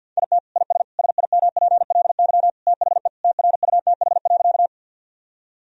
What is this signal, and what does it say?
Morse code, heard as a keyed tone, about 46 wpm, 700 Hz. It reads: IT IS SIMPLY THE TRUTH.
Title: Sentences from Top 1000 Words - Review Part 4 - 50wpm